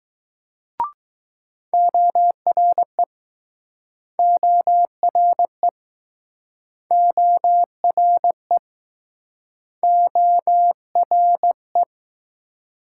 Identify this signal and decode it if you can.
Morse code, heard as a keyed tone, about 20 wpm, 700 Hz. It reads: ORE ORE ORE ORE